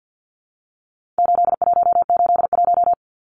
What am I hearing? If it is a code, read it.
Morse code, heard as a keyed tone, about 50 wpm, 700 Hz. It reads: 8181
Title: Numbers - 4-Digit 50wpm